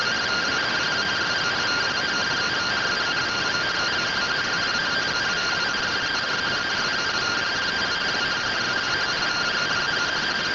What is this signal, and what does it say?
Two-tone teleprinter signal, about 170 Hz shift.
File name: findings / CT AXIAL -111.gen-19 rtty rx